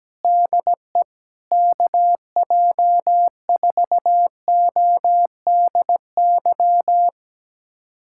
Morse code, heard as a keyed tone, about 17 wpm, 700 Hz. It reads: DE KJ4ODY
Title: USA Callsign DE Callsign 17wpm